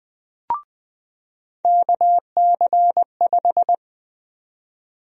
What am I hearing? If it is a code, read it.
Morse code, heard as a keyed tone, about 20 wpm, 700 Hz. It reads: KC5
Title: USA Call Sign Prefixes - 20wpm